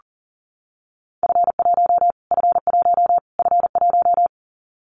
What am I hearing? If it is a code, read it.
Morse code, heard as a keyed tone, about 40 wpm, 700 Hz. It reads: F1 F1 F1
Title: Callsign Prefixes for Top 25 DXCC - Repeated 3x - 40wpm